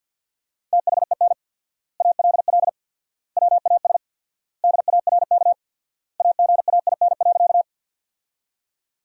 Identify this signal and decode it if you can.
Morse code, heard as a keyed tone, about 47 wpm, 700 Hz. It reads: THEN ALL WAS DARK AGAIN.
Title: 4x Word-Spacing - Sentences from Top 500 Words - 50wpm